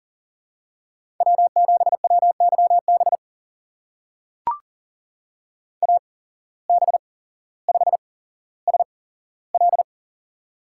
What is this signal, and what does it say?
Morse code, heard as a keyed tone, about 40 wpm, spaced as for 17 wpm, 700 Hz. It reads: W7WYB AB5SL
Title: USA Call Signs - 5 Characters - 8x Character Spacing - 40wpm